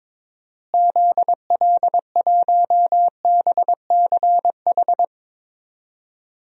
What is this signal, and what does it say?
Morse code, heard as a keyed tone, about 22 wpm, 700 Hz. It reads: ZL1BCH
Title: DXCC-100 Call Signs - 6 Characters - Repeated 2x - 22wpm